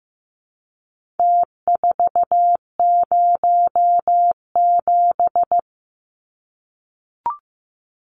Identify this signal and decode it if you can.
Morse code, heard as a keyed tone, about 15 wpm, 700 Hz. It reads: T407
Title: Numbers - Contest Style - 4-Digit 15wpm